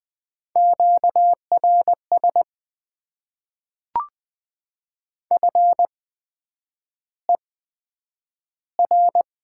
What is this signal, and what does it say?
Morse code, heard as a keyed tone, about 20 wpm, spaced as for 9 wpm, 700 Hz. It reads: QRS FER
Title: Top 90 QSO Elements - 8x Character Spacing - 20wpm